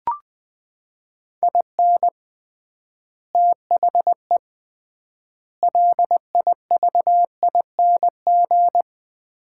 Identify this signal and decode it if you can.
Morse code, heard as a keyed tone, about 20 wpm, 700 Hz. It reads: IN THE LIVING
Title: 3x Word-Spacing - Sets of 4 Words - 20wpm